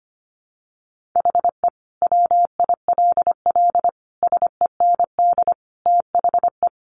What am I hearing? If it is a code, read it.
Morse code, heard as a keyed tone, about 25 wpm, 700 Hz. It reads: HE WILL SEND THE
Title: Additional Easy Phrases 25wpm